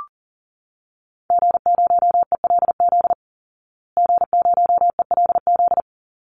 Morse code, heard as a keyed tone, about 40 wpm, 700 Hz. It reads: G0ELZ G0ELZ
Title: Call Signs Repeated 3x - 40wpm